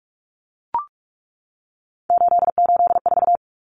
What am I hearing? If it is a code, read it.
Morse code, heard as a keyed tone, about 44 wpm, 700 Hz. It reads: 884
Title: Common Cut Numbers - 3-Digit 45wpm